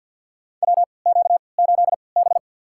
Morse code, heard as a keyed tone, about 52 wpm, spaced as for 29 wpm, 700 Hz. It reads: WQ8B
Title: USA Call Signs - 4 Characters - 3x Character Spacing - 50wpm